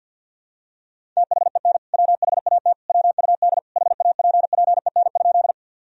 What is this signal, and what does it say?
Morse code, heard as a keyed tone, about 51 wpm, 700 Hz. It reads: THEN WHAT WUD HAPPEN?
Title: Sentences from Top 400 Words and Abbreviations 50wpm